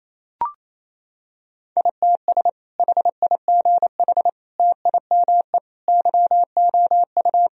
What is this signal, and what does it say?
Morse code, heard as a keyed tone, about 28 wpm, 700 Hz. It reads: ITS HIGH TIME YOU
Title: Sentences from Top 1000 Words - Review Part 1 - 28wpm